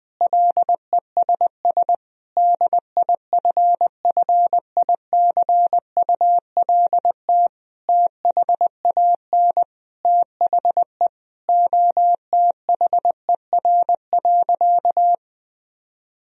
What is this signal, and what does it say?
Morse code, heard as a keyed tone, about 20 wpm, 700 Hz. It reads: LESS DIFFICULT THAN THE OTHER.
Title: Sentences from Top 1000 Words - Review Part 2 - 20wpm